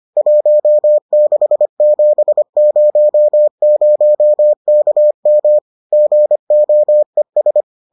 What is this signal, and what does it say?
Morse code, heard as a keyed tone, about 25 wpm, 600 Hz. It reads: 16700 KM GOES